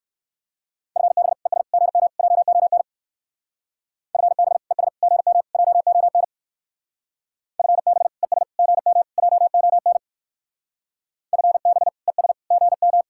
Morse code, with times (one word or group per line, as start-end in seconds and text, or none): 1.0-1.3 FB
1.5-1.6 ES
1.7-2.1 GM
2.2-2.8 JON
4.1-4.6 FB
4.7-4.9 ES
5.0-5.4 GM
5.5-6.2 JON
7.6-8.1 FB
8.2-8.4 ES
8.6-9.0 GM
9.2-10.0 JON
11.3-11.9 FB
12.1-12.3 ES
12.5-13.0 GM